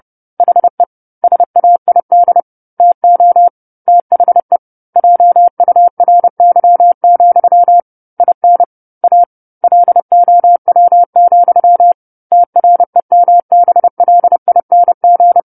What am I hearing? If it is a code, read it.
Morse code, heard as a keyed tone, about 30 wpm, 700 Hz. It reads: HE SAID TO THE JURY, IN A LOW, TREMBLING
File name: 10-Chapter-10 30 WPM